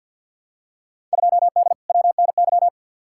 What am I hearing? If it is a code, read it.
Morse code, heard as a keyed tone, about 52 wpm, 700 Hz. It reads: 2D WNY
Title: Field Day Exchanges - 50wpm